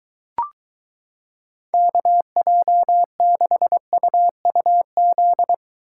Morse code, heard as a keyed tone, about 23 wpm, 700 Hz. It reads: KJ6UUZ